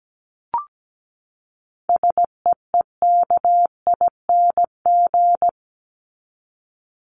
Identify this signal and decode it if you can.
Morse code, heard as a keyed tone, about 17 wpm, 700 Hz. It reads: SEEKING
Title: ING Suffix - Top 100 Words with Base Words 17wpm